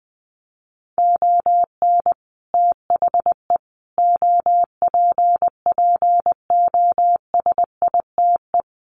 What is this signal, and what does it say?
Morse code, heard as a keyed tone, about 20 wpm, 700 Hz. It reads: ON THE OPPOSITE
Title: Sets of 5 Words Encore2 20wpm